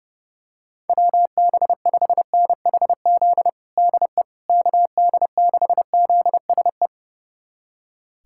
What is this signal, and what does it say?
Morse code, heard as a keyed tone, about 30 wpm, 700 Hz. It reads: WB5NHZ DE KD6ZSE
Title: USA Callsign DE Callsign 30wpm